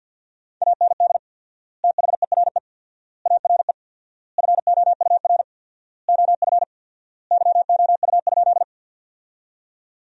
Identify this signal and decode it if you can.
Morse code, heard as a keyed tone, about 46 wpm, 700 Hz. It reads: AND THERE ARE FOUR OF YOU?
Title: 4x Word-Spacing - Sentences from Top 300 Words - 50wpm